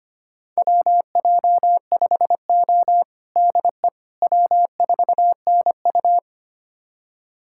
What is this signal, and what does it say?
Morse code, heard as a keyed tone, about 25 wpm, 700 Hz. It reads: WJ5O DE W4NU
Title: USA Callsign DE Callsign - 4 Characters - 25wpm